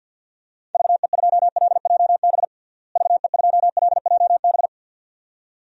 Morse code, heard as a keyed tone, about 52 wpm, 700 Hz. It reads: VE2LJB VE2LJB
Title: DXCC-100 Call Signs - 6 Characters - Repeated 2x - 50wpm